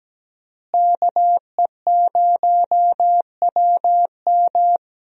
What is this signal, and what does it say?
Morse code, heard as a keyed tone, about 17 wpm, 700 Hz. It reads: KE0WM
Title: USA Call Signs - 5 Characters - Repeated 3x - 17wpm